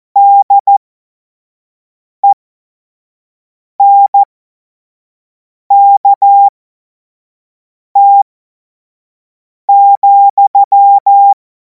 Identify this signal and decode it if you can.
Morse code, heard as a keyed tone, about 14 wpm, spaced as for 5 wpm, 800 Hz. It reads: DENKT,
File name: Morsekurs / Witze / Witz 17. März 2026 25 F70